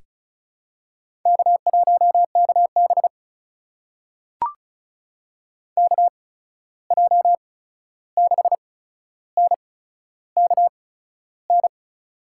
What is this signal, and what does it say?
Morse code, heard as a keyed tone, about 36 wpm, spaced as for 13 wpm, 700 Hz. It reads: K1KB KJ6NKN
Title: USA Call Signs - 8x Character Spacing - 35wpm